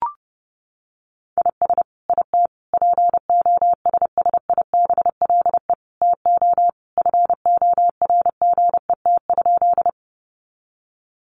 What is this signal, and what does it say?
Morse code, heard as a keyed tone, about 28 wpm, 700 Hz. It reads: IS IT POSSIBLE TO FORGET?
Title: Sentences from Top 1200 Words 30wpm